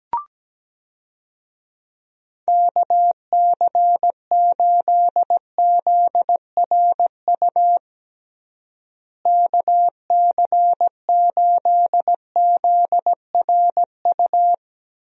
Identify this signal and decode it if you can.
Morse code, heard as a keyed tone, about 17 wpm, 700 Hz. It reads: KC8ZRU KC8ZRU
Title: USA Call Signs - 6 Characters - Repeated 3x - 17wpm